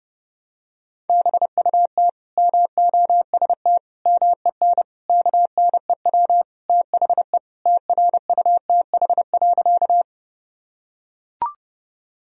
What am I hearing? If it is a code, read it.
Morse code, heard as a keyed tone, about 32 wpm, 700 Hz. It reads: BUT MOST MEN KNEW THE TRUTH.
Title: Sentences from Top 1000 Words - Review Part 1 - 30wpm